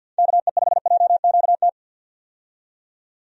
Morse code, heard as a keyed tone, about 51 wpm, 700 Hz. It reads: KE5JQT
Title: USA Call Signs - 6 Characters - 50wpm